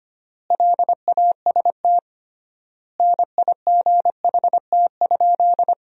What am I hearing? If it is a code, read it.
Morse code, heard as a keyed tone, about 25 wpm, 700 Hz. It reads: LAST NIGHT?
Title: 3x Word-Spacing - Sentences from Top 300 Words - 25wpm